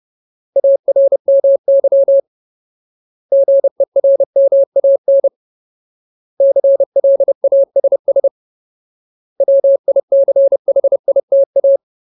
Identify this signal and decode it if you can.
Morse code, heard as a keyed tone, about 30 wpm, 550 Hz. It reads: ARMY GERMAN CLASS WICHITA
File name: Ultimate Word List / 30WPM Ultimate